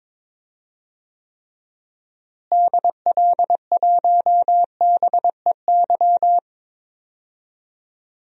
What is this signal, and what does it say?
Morse code, heard as a keyed tone, about 22 wpm, 700 Hz. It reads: DL1BEY
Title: DXCC-100 Call Signs - 6 Characters - Repeated 2x - 22wpm